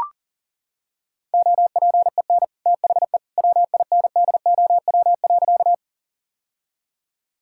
Morse code, heard as a keyed tone, about 40 wpm, 700 Hz. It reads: OPEN THE WINDOW.
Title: Sentences from Top 1600 Words 40wpm